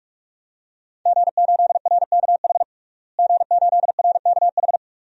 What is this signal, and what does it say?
Morse code, heard as a keyed tone, about 46 wpm, 700 Hz. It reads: G8RKH G8RKH
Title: DXCC-100 Call Signs - 5 Characters - Repeated 2x - 45wpm